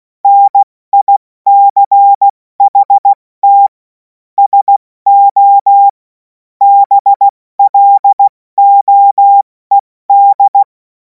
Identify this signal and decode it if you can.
Morse code, heard as a keyed tone, about 16 wpm, 800 Hz. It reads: NICHT SO BLOED
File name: Morsekurs / Witze / Witz 17. Februar 2026 70 F80